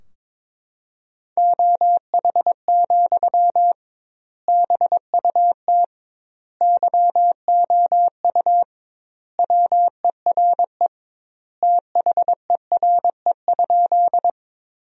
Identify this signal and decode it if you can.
Morse code, heard as a keyed tone, about 22 wpm, 700 Hz. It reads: OH, BUT YOU WERE THERE?
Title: 2x Word-Spacing - Sentences from Top 500 Words - 22wpm